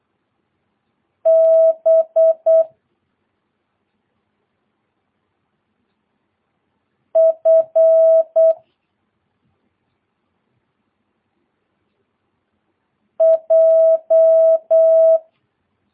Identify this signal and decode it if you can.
Morse code, heard as a keyed tone, about 8 wpm, 650 Hz. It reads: B F J
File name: jh8cbh / 2022cw